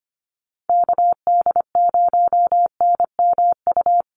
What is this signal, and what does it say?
Morse code, heard as a keyed tone, about 25 wpm, 700 Hz. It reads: KD0NMU
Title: USA Call Signs Repeated 2x 25wpm